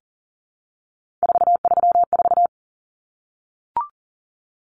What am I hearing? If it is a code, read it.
Morse code, heard as a keyed tone, about 40 wpm, 700 Hz. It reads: 434
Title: Common Cut Numbers - 3-Digit 40wpm